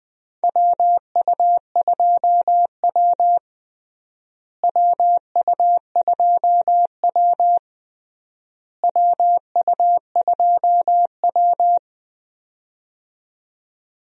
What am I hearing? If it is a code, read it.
Morse code, heard as a keyed tone, about 21 wpm, 700 Hz. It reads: WU2W WU2W WU2W
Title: USA Call Signs - 4 Characters - Repeated 3x - 20wpm